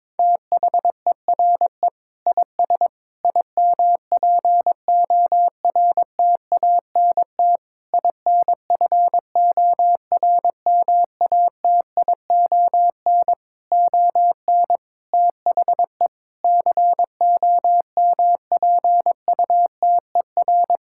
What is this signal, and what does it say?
Morse code, heard as a keyed tone, about 22 wpm, 700 Hz. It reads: THERE IS IMPORTANT INFORMATION ON THE COMPUTER
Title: Sentences from Top 1000 Words - Review Part 6 - 22wpm